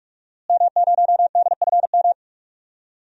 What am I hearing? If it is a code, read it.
Morse code, heard as a keyed tone, about 46 wpm, 700 Hz. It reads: M0DFM
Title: DXCC-100 Call Signs - 5 Characters - Repeated 3x - 45wpm